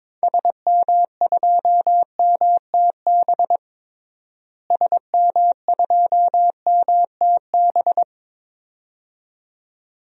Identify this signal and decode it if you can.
Morse code, heard as a keyed tone, about 22 wpm, 700 Hz. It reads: SM2MTB SM2MTB
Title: DXCC-100 Call Signs - 6 Characters - Repeated 2x - 22wpm